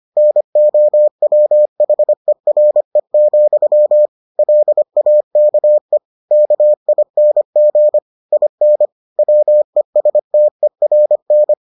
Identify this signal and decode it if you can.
Morse code, heard as a keyed tone, about 25 wpm, 600 Hz. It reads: NOWHERE, LAKE KING IN WESTERN